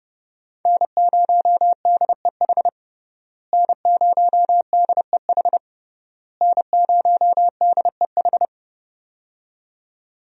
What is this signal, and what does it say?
Morse code, heard as a keyed tone, about 30 wpm, 700 Hz. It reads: N0DEH N0DEH N0DEH
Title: USA Call Signs - 5 Characters - Repeated 3x - 30wpm